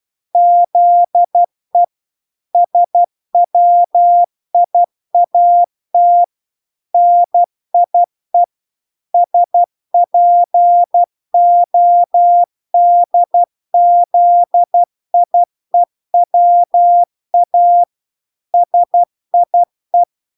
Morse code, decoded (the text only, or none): ZE SWIAT NIE SPODZIEWA SIE